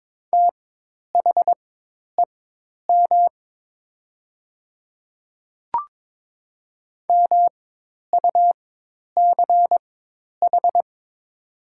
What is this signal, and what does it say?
Morse code, heard as a keyed tone, about 22 wpm, spaced as for 10 wpm, 700 Hz. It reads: THEM MUCH
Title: 4x Character Spacing - Top 200 Words - 22wpm